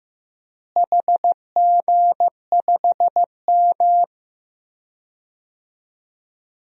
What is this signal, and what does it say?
Morse code, heard as a keyed tone, about 15 wpm, 700 Hz. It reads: HG5M
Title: DXCC-100 Call Signs - 4 Characters - Repeated 2x - 15wpm